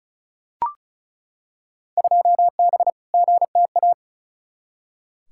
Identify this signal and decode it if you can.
Morse code, heard as a keyed tone, about 34 wpm, 700 Hz. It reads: 2B GTA
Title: Field Day Exchanges Encore - 35wpm